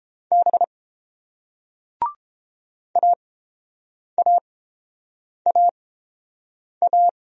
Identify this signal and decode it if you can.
Morse code, heard as a keyed tone, about 30 wpm, spaced as for 7 wpm, 700 Hz. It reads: B AAAA